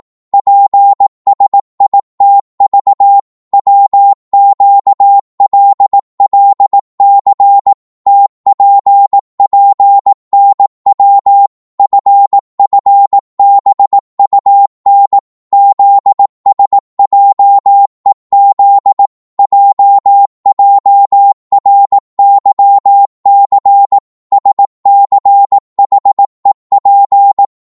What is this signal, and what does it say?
Morse code, heard as a keyed tone, about 18 wpm, 800 Hz. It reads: PSITV WQLLC TPPNW FFBUN ZSJEZ JJRYC SCHEP